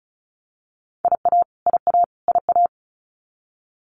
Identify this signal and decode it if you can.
Morse code, heard as a keyed tone, about 34 wpm, 700 Hz. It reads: IA IA IA
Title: US States Repeated 3x 35wpm